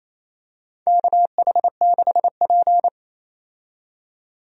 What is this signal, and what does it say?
Morse code, heard as a keyed tone, about 28 wpm, 700 Hz. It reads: KH6P
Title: DXCC-100 Call Signs - 4 Characters - 28wpm